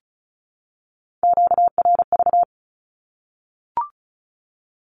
Morse code, heard as a keyed tone, about 34 wpm, 700 Hz. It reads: QRV
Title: Q-Codes 35wpm